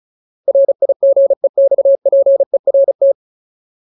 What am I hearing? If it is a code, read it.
Morse code, heard as a keyed tone, about 34 wpm, 550 Hz. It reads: RIGEXPERT
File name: Ultimate Word List / 35WPM Ultimate